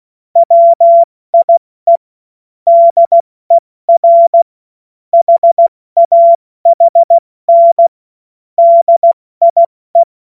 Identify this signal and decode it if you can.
Morse code, heard as a keyed tone, about 16 wpm, spaced as for 14 wpm, 680 Hz. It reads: WIE DER HAHN DIE
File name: Witz-08-Juli-2025-70-F80